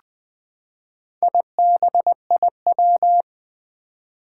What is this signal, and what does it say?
Morse code, heard as a keyed tone, about 20 wpm, 700 Hz. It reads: IBIW
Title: Four Challenging Letter-Number Combinations 20wpm